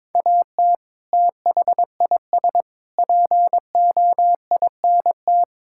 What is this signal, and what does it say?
Morse code, heard as a keyed tone, about 22 wpm, 700 Hz. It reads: AT THIS POINT
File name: practice / 22wpm / Sentences with 3-Word Phrases - 22wpm